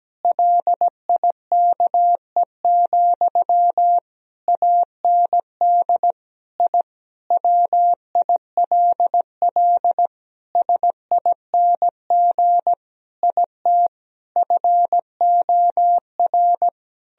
Morse code, heard as a keyed tone, about 17 wpm, 700 Hz. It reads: LIKE, AND I WILL SING IT FOR
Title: Sentences from Top 1000 Words - Review Part 2 - 17wpm